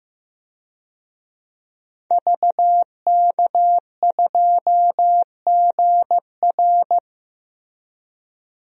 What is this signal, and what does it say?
Morse code, heard as a keyed tone, about 15 wpm, 700 Hz. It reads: VK2GR
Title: DXCC-100 Call Signs - 5 Characters - Repeated 3x - 15wpm